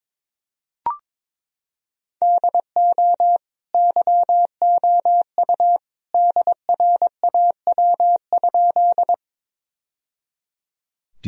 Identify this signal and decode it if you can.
Morse code, heard as a keyed tone, about 22 wpm, 700 Hz. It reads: DO YOU DRAW?
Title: Sentences from Top 1000 Words - Review Part 1 - 22wpm